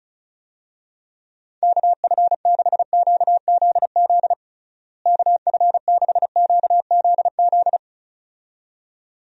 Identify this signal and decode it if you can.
Morse code, heard as a keyed tone, about 36 wpm, 700 Hz. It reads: KF6QZZ KF6QZZ
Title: USA Call Signs - 6 Characters - Repeated 2x - 35wpm